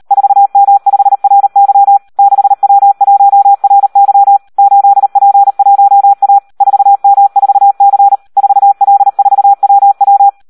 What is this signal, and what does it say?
Morse code, heard as a keyed tone, about 38 wpm, 800 Hz. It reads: 4M5RY 6W1RY 8P1A 4M4C 4L4WW
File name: Audio practica CW / SD 1 38